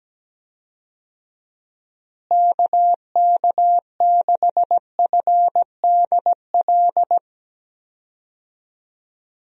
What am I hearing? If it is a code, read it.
Morse code, heard as a keyed tone, about 17 wpm, 700 Hz. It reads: KK6FDL
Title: USA Call Signs - 6 Characters - Repeated 3x - 17wpm